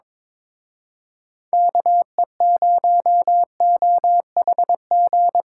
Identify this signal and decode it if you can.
Morse code, heard as a keyed tone, about 25 wpm, 700 Hz. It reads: KE0OHG
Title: Speed Racing - USA 2x3 Call Signs 22wpm